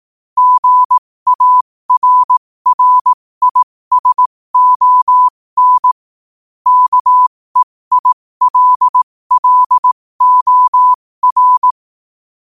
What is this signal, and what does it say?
Morse code, heard as a keyed tone, about 18 wpm, spaced as for 15 wpm, 1000 Hz. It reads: GARRISON KEILLOR
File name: qotdfiles / 15wpm